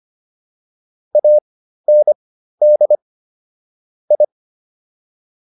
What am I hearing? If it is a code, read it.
Morse code, heard as a keyed tone, about 25 wpm, spaced as for 13 wpm, 600 Hz. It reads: AND I